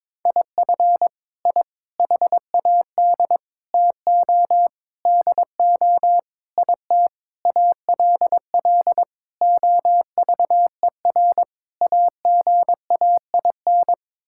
Morse code, heard as a keyed tone, about 22 wpm, 700 Hz. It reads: IF I HAD TO DO IT ALL OVER AGAIN
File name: Sets of 9 Words - 22wpm